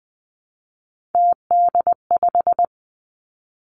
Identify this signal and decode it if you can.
Morse code, heard as a keyed tone, about 19 wpm, 700 Hz. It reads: TD5
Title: Three Challenging Letter-Number Combinations 20wpm